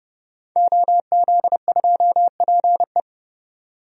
Morse code, heard as a keyed tone, about 31 wpm, 700 Hz. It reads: OZ2PE
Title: DXCC-100 Call Signs - 5 Characters - 30wpm